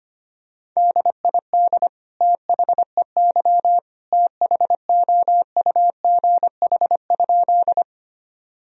Morse code, heard as a keyed tone, about 25 wpm, 700 Hz. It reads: DID THEY THOUGH?
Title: Sentences from Top 1000 Words - Review Part 2 - 25wpm